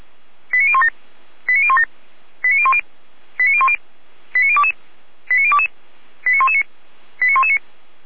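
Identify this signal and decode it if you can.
Five-tone selective call (ZVEI), heard as quick runs of five tones, tones of about 70 ms. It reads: 89017 89017 89010 89010 89022 89022 89119 89119
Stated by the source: ZVEI five-tone selective call received by SDR.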